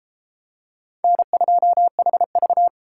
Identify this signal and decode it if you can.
Morse code, heard as a keyed tone, about 29 wpm, 700 Hz. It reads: N2HV